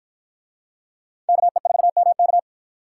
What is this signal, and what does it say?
Morse code, heard as a keyed tone, about 52 wpm, 700 Hz. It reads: KE4MK